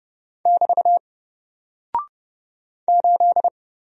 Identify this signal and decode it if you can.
Morse code, heard as a keyed tone, about 30 wpm, 700 Hz. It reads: <BT> 8
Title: Single Character (Letter:Number:Punctuation:Prosign) Alt 30wpm